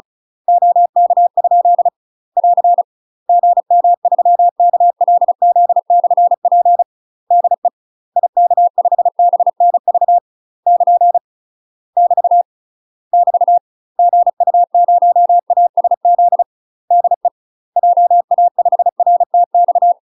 Morse code, heard as a keyed tone, about 36 wpm, 700 Hz. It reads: OK? <AR> GM3KLZ/P DE IK5BNV <KN> <BT> <BT> GU0ASZ DE JA5RTX
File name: UK MASTER QSOs TRAINING / 35WPM / UK MASTER QSOs 0041